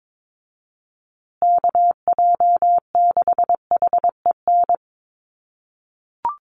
Morse code, heard as a keyed tone, about 22 wpm, 700 Hz. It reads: KJ6HEN
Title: USA Call Signs Repeated 2x 22wpm